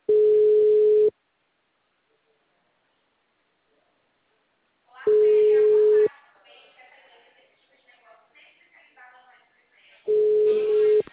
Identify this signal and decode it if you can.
Ringback tone (Europe, 425 Hz), with speech.